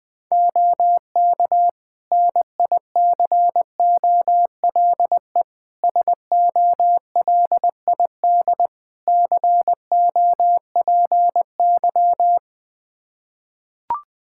Morse code, heard as a keyed tone, about 21 wpm, 700 Hz. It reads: OK NICOLE SOLID COPY
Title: Practice QSO - Exchange 3 - More Info - 20wpm